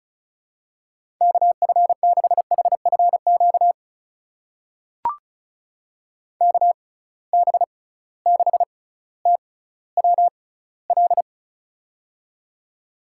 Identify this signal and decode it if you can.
Morse code, heard as a keyed tone, about 36 wpm, spaced as for 18 wpm, 700 Hz. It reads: KF6HFQ KB6TWL
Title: USA Call Signs - 6 Characters - 6x Character Spacing - 35wpm